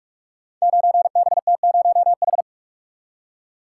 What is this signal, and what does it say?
Morse code, heard as a keyed tone, about 44 wpm, 700 Hz. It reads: 9BT0H
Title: Code Groups - 5 Letter:Numbers - 45wpm